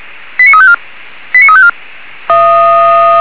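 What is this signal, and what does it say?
Five-tone selective call (ZVEI), heard as quick runs of five tones, tones of about 70 ms. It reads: 89354 89354
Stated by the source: ZVEI five-tone selective call received on an SDR.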